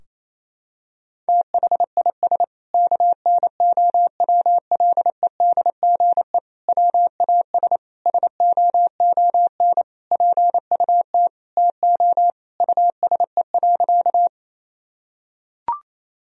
Morse code, heard as a keyed tone, about 28 wpm, 690 Hz. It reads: THIS KNOWLEDGE WAS SOON PUT TO USE.
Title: Sentences from Top 1000 Words - Review Part 3 - 28wpm